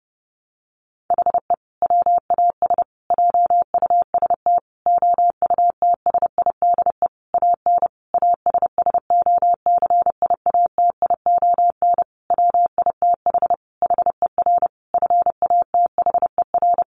Morse code, heard as a keyed tone, about 30 wpm, 700 Hz. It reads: HE WAS JUST OUTSIDE AN ASSOCIATION WITH HER FATHER
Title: Sentences from Top 1100 Words 30wpm